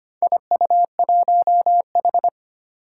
Morse code, heard as a keyed tone, about 25 wpm, 700 Hz. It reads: IU1H
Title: Four Challenging Letter-Number Combinations 25wpm